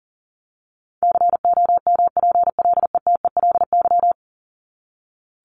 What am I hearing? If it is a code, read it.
Morse code, heard as a keyed tone, about 40 wpm, 700 Hz. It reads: COMPLETELY
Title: LY Suffix - Top 100 Words with Base Words 40wpm